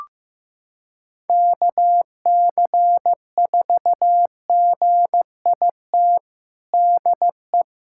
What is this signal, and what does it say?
Morse code, heard as a keyed tone, about 16 wpm, 700 Hz. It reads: KC4GIT DE
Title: USA Callsign DE Callsign 15wpm